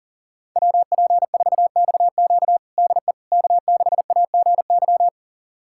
Morse code, heard as a keyed tone, about 40 wpm, 700 Hz. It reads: WP4XQ DE K6AGY
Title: USA Callsign DE Callsign 40wpm